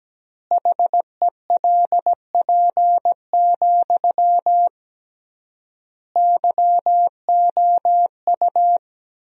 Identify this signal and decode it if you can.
Morse code, heard as a keyed tone, about 17 wpm, 700 Hz. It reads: HELP, YOU
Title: 3x Word-Spacing - Sentences from Top 300 Words - 17wpm